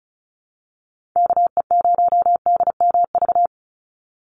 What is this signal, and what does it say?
Morse code, heard as a keyed tone, about 34 wpm, 700 Hz. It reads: KE0DMV
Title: USA Call Signs 35wpm